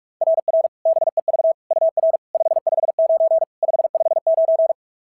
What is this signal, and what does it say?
Morse code, heard as a keyed tone, about 44 wpm, 690 Hz. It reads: RR BEV UR 559 559
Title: SOTA QSO Practice 45wpm